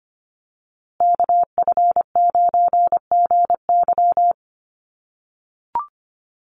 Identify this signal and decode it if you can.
Morse code, heard as a keyed tone, about 25 wpm, 700 Hz. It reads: KF9GY
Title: USA Call Signs Repeated 2x 25wpm